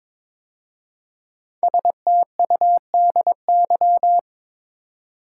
Morse code, heard as a keyed tone, about 23 wpm, 700 Hz. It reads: STUDY
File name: 5 Letter Words - First 3 Letters Given - 22wpm